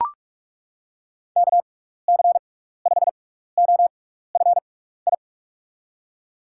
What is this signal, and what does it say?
Morse code, heard as a keyed tone, about 41 wpm, spaced as for 15 wpm, 700 Hz. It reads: KC5OFI
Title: USA Call Signs - 6 Characters - 6x Character Spacing - 45wpm